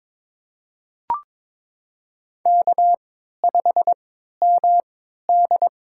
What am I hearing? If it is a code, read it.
Morse code, heard as a keyed tone, about 22 wpm, spaced as for 13 wpm, 700 Hz. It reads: K5MD
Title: USA Call Signs - 4 Characters - 3x Character Spacing - 22wpm